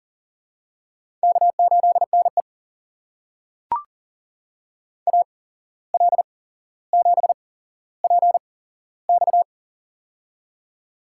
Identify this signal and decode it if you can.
Morse code, heard as a keyed tone, about 37 wpm, spaced as for 16 wpm, 700 Hz. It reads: K8NE AL7PX
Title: USA Call Signs - 8x Character Spacing - 40wpm